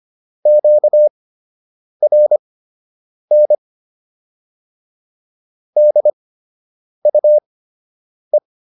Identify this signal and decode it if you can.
Morse code, heard as a keyed tone, about 25 wpm, spaced as for 8 wpm, 600 Hz. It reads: QRN DUE